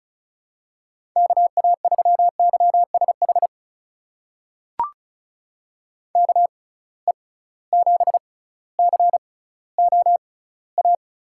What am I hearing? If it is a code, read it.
Morse code, heard as a keyed tone, about 34 wpm, spaced as for 18 wpm, 700 Hz. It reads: KA3YSH KE7COA
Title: USA Call Signs - 6 Characters - 6x Character Spacing - 35wpm